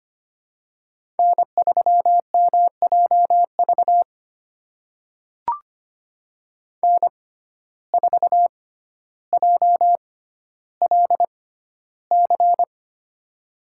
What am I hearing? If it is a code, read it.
Morse code, heard as a keyed tone, about 25 wpm, spaced as for 13 wpm, 700 Hz. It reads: N3MJV N4JLC
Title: USA Call Signs - 5 Characters - 6x Character Spacing - 25wpm